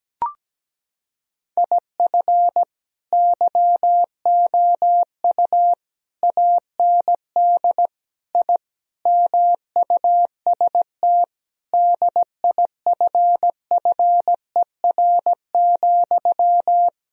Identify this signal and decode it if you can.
Morse code, heard as a keyed tone, about 17 wpm, 700 Hz. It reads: IF YOU AND I MUST DIFFER,
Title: Sentences from Top 1000 Words - Review Part 1 - 17wpm